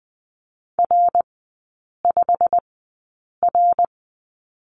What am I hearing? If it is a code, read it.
Morse code, heard as a keyed tone, about 20 wpm, spaced as for 8 wpm, 700 Hz. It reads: R5R
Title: 138 Lesson 23e - Letter f - Three Characters 20wpm